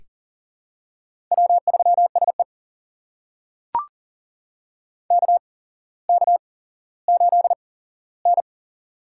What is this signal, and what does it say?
Morse code, heard as a keyed tone, about 38 wpm, spaced as for 17 wpm, 700 Hz. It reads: W3SE KK8N
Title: USA Call Signs - 4 Characters - 8x Character Spacing - 40wpm